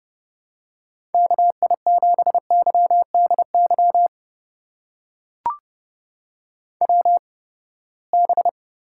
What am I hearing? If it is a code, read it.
Morse code, heard as a keyed tone, about 30 wpm, spaced as for 21 wpm, 700 Hz. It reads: KI7YDY WB
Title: USA Call Signs - 6 Characters - 8x Character Spacing - 30wpm